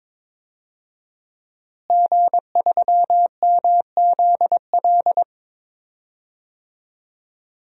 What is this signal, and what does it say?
Morse code, heard as a keyed tone, about 21 wpm, 700 Hz. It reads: G3MZL
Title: DXCC-100 Call Signs - 5 Characters - Repeated 3x - 22wpm